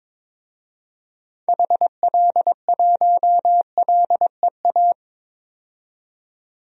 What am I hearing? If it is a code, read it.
Morse code, heard as a keyed tone, about 21 wpm, 700 Hz. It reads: HL1LEA
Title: DXCC-100 Call Signs - 6 Characters - 22wpm